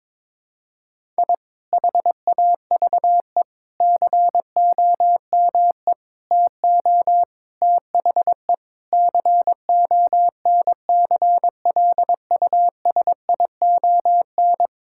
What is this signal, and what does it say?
Morse code, heard as a keyed tone, about 22 wpm, 700 Hz. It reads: I HAVE COME TO THE CONCLUSION
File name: Sentences with 7-Word Phrases - 22wpm